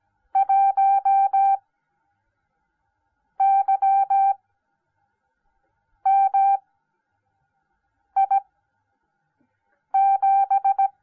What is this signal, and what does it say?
Morse code, heard as a keyed tone, about 17 wpm, spaced as for 5 wpm, 780 Hz. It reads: 1YMI7